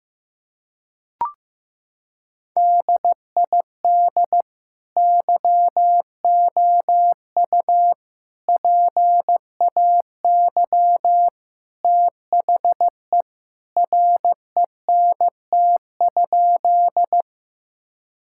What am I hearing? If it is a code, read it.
Morse code, heard as a keyed tone, about 15 wpm, 700 Hz. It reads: DID YOU PAY THE RENT?
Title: Sentences from Top 1000 Words - Review Part 5 - 15wpm